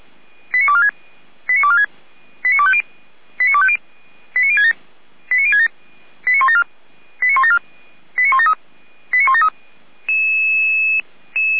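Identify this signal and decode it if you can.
Five-tone selective call (ZVEI), heard as quick runs of five tones, tones of about 70 ms. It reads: 89246 89246 89250 89250 89067 89067 89174 89174 89173 89173
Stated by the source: ZVEI five-tone selective call received by SDR, fire brigade Feuerwehr Orlamünde.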